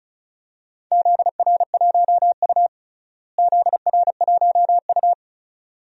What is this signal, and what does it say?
Morse code, heard as a keyed tone, about 34 wpm, 700 Hz. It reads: ZR1U ZR1U
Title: DXCC-100 Call Signs - 4 Characters - Repeated 2x - 35wpm